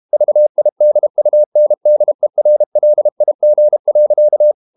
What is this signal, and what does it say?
Morse code, heard as a keyed tone, about 32 wpm, 600 Hz. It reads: VIDUNDERLIG.
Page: Ferie 32 wpm